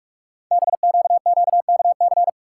Morse code, heard as a keyed tone, about 44 wpm, 700 Hz. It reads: BQQKC
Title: Code Groups - 5 Letters - 45wpm